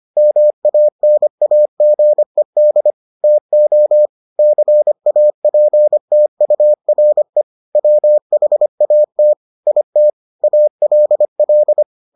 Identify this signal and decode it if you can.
Morse code, heard as a keyed tone, about 25 wpm, 600 Hz. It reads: MANAGED TO CAPTURE WHAT IT ALL